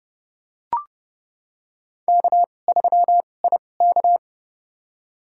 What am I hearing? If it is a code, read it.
Morse code, heard as a keyed tone, about 30 wpm, spaced as for 22 wpm, 690 Hz. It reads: K3IK
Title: USA Call Signs - 4 Characters - 2x Character Spacing - 30wpm